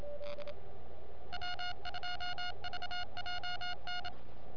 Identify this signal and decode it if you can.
Morse code, heard as a keyed tone, about 28 wpm, 1430 Hz. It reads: W2VJN
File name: n6tr / K5TR JARun